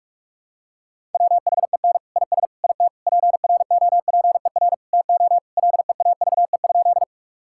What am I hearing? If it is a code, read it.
Morse code, heard as a keyed tone, about 46 wpm, 700 Hz. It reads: WHEN IS IT PROPER TO LEAVE?
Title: Sentences from Top 1000 Words - Review Part 5 - 45wpm